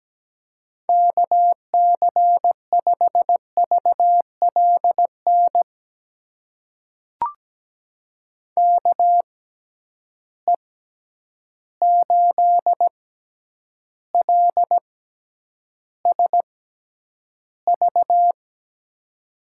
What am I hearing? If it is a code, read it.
Morse code, heard as a keyed tone, about 17 wpm, spaced as for 9 wpm, 700 Hz. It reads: KC5VLN KE8LSV
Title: USA Call Signs - 6 Characters - 6x Character Spacing - 17wpm